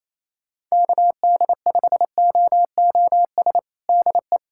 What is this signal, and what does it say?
Morse code, heard as a keyed tone, about 28 wpm, 700 Hz. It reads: KD5OOS DE
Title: USA Callsign DE Callsign 28wpm